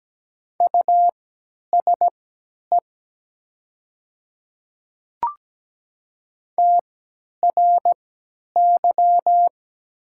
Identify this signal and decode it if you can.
Morse code, heard as a keyed tone, about 17 wpm, spaced as for 10 wpm, 700 Hz. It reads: USE TRY
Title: 3x Character Spacing - Top 300 Words - 17wpm